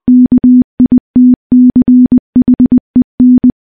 Morse code, heard as a keyed tone, about 20 wpm, 250 Hz. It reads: KITCHEN
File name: morse / morse-training-20WPM